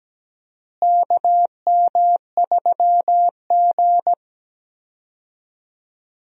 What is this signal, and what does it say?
Morse code, heard as a keyed tone, about 17 wpm, 700 Hz. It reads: KM3G
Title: USA Call Signs - 4 Characters - Repeated 2x - 17wpm